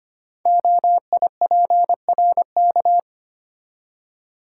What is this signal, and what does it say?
Morse code, heard as a keyed tone, about 25 wpm, 700 Hz. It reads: OIPRK
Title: Code Groups - 5 Letters - 25wpm